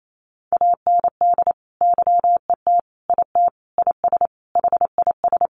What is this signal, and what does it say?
Morse code, heard as a keyed tone, about 28 wpm, 700 Hz. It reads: AND YET IT IS HIS
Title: Sentences from Top 800 Words 28wpm